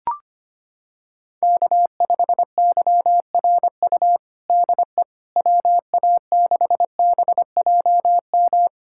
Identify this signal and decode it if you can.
Morse code, heard as a keyed tone, about 25 wpm, 700 Hz. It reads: K5YRU DE WA6BJM
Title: USA Callsign DE Callsign 25wpm